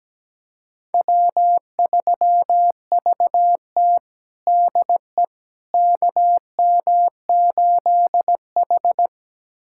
Morse code, heard as a keyed tone, about 17 wpm, 700 Hz. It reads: W3VT DE KM8H
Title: USA Callsign DE Callsign - 4 Characters - 17wpm